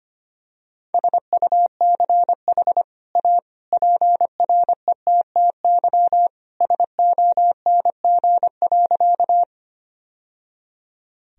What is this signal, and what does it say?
Morse code, heard as a keyed tone, about 25 wpm, 700 Hz. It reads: SUCH A PRETTY SONG.
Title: Sentences from Top 1000 Words - Review Part 2 - 25wpm